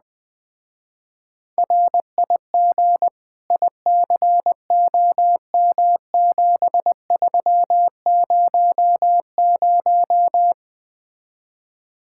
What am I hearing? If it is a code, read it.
Morse code, heard as a keyed tone, about 20 wpm, 700 Hz. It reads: RIG ICOM7300